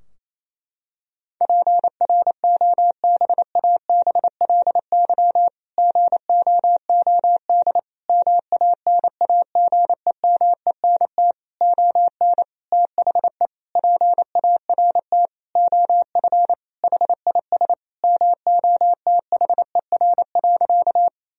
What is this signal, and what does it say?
Morse code, heard as a keyed tone, about 28 wpm, 700 Hz. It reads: PROBABLY GOOD MANAGEMENT ON THE PART OF HIS MOTHER.
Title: Sentences from Top 1000 Words - Review Part 4 - 28wpm